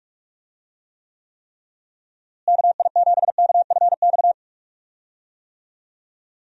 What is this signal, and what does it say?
Morse code, heard as a keyed tone, about 46 wpm, 700 Hz. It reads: KI7KFX
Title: USA Call Signs - 6 Characters - Repeated 3x - 45wpm